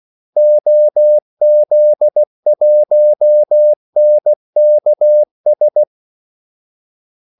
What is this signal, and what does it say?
Morse code, heard as a keyed tone, about 16 wpm, 600 Hz. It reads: OZ1NKS
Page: Kaldesignaler 16 wpm